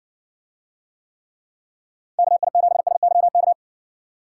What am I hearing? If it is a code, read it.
Morse code, heard as a keyed tone, about 60 wpm, 700 Hz. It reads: KI7SYK